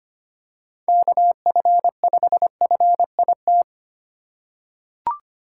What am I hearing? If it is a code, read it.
Morse code, heard as a keyed tone, about 25 wpm, 700 Hz. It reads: KF5FIT
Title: USA Call Signs - 6 Characters - 4x Character Spacing - 25wpm